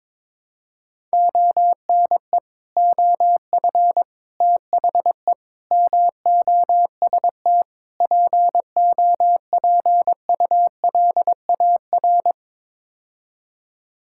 Morse code, heard as a keyed tone, about 21 wpm, 700 Hz. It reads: ONE OF THE MOST POPULAR
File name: practice / 22wpm / Build A Phrase - 5-Word Phrases - 22wpm